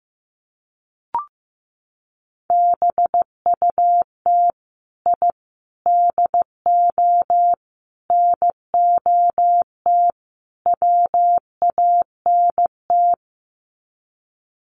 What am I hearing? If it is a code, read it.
Morse code, heard as a keyed tone, about 15 wpm, 700 Hz. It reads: BUT I DO NOT WANT
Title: Sets of 5 Words 15wpm